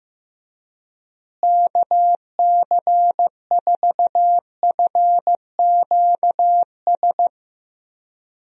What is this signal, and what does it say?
Morse code, heard as a keyed tone, about 15 wpm, 700 Hz. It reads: KC4FQS
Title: USA Call Signs - 6 Characters - Repeated 3x - 15wpm